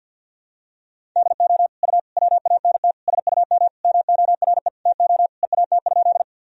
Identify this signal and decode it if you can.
Morse code, heard as a keyed tone, about 50 wpm, 700 Hz. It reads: DO U WANT SUM MORE TO EAT?
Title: Sentences from Top 300 Words and Abbreviations 50wpm